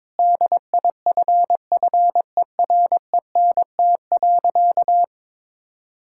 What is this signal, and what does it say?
Morse code, heard as a keyed tone, about 22 wpm, 700 Hz. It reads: DIFFERENT.
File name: practice / 22wpm / Sentences with 8-Word Phrases - 22wpm